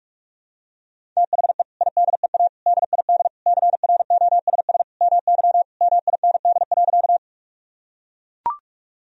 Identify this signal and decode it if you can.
Morse code, heard as a keyed tone, about 42 wpm, 700 Hz. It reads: THE IDEA DID CROSS MY MIND.
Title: Sentences from Top 1000 Words - Review Part 1 - 45wpm